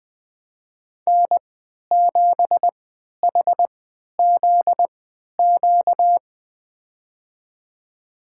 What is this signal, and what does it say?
Morse code, heard as a keyed tone, about 20 wpm, spaced as for 11 wpm, 700 Hz. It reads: N7HZQ
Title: USA Call Signs - 5 Characters - 3x Character Spacing - 20wpm